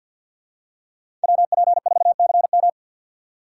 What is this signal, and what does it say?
Morse code, heard as a keyed tone, about 49 wpm, 700 Hz. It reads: WP4CM
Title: USA Call Signs - 3x Character Spacing - 50wpm